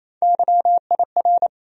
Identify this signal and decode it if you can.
Morse code, heard as a keyed tone, about 28 wpm, 700 Hz. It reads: YIR
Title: USA Call Sign Postfixes - 3 Letters 28wpm